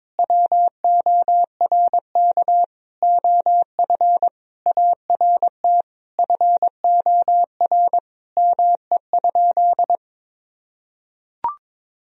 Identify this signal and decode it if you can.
Morse code, heard as a keyed tone, about 22 wpm, 700 Hz. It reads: WORK OF ART FOR ME?
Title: Sentences from Top 1000 Words - Review Part 3 - 22wpm